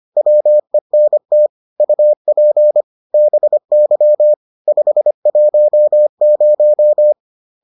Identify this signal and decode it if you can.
Morse code, heard as a keyed tone, about 25 wpm, 600 Hz. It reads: WENT UP BY 510